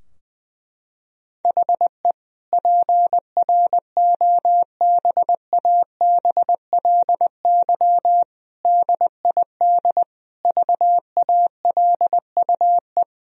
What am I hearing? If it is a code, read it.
Morse code, heard as a keyed tone, about 20 wpm, 700 Hz. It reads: HE PROBABLY DID VALUE
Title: Sentences from Top 1000 Words - Review Part 4 - 20wpm